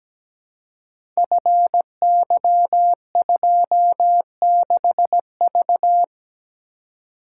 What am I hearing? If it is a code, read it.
Morse code, heard as a keyed tone, about 17 wpm, 700 Hz. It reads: FY26V
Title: Code Groups - 5 Letter:Numbers - 17wpm